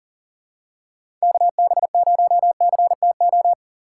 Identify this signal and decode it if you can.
Morse code, heard as a keyed tone, about 39 wpm, 700 Hz. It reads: KB0CTO
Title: USA Call Signs - 6 Characters - 4x Character Spacing - 40wpm